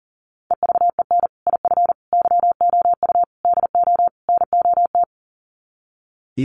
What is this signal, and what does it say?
Morse code, heard as a keyed tone, about 40 wpm, 700 Hz. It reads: EVEN IF YOU DO NOT
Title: Sets of 5 Words Encore 40wpm